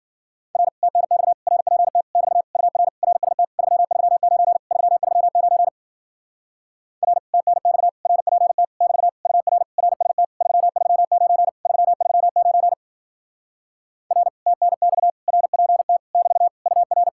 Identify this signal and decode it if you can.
Morse code, heard as a keyed tone, about 53 wpm, 700 Hz. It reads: R TNX RPT <BT> UR RST 339 339 R TNX RPT <BT> UR RST 339 339 R TNX RPT <BT> UR